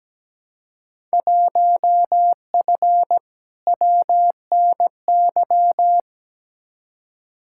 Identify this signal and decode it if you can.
Morse code, heard as a keyed tone, about 17 wpm, 700 Hz. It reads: 1F WNY
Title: Field Day Exchanges - 17wpm